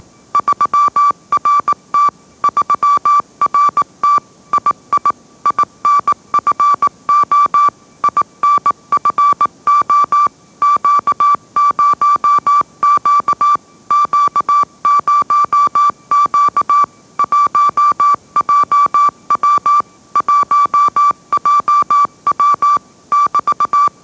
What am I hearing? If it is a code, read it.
Morse code, heard as a keyed tone, about 19 wpm, 1160 Hz. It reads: 3RT 3RT II INFO INFO Q0Q Q0Q 1JW 1JW <BT>